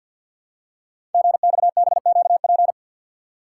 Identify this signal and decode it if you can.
Morse code, heard as a keyed tone, about 52 wpm, 700 Hz. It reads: GXBQP
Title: Code Groups - 5 Letters - 50wpm